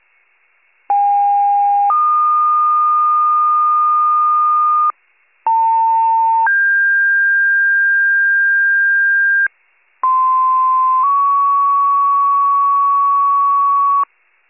Two-tone page, heard as a one-second tone, then a three-second tone, about 810 then 1220 Hz, about 900 then 1600 Hz, about 1040 then 1120 Hz. It is heard from 0.9 to 14.0 s.